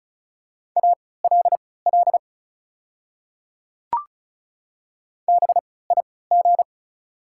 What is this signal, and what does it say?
Morse code, heard as a keyed tone, about 36 wpm, spaced as for 20 wpm, 700 Hz. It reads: ALL BIG
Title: 3x Character Spacing - Top 200 Words - 35wpm